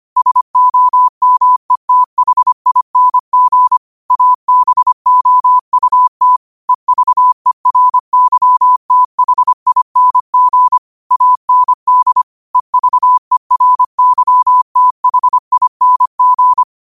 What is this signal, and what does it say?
Morse code, heard as a keyed tone, about 25 wpm, 1000 Hz. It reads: SOMETHING ABOUT EVERYTHING AND EVERYTHING
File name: qotdfiles / 25wpm